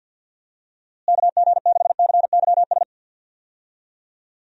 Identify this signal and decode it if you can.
Morse code, heard as a keyed tone, about 52 wpm, 700 Hz. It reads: KG6CYS
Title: USA Call Signs - 6 Characters - Repeated 2x - 50wpm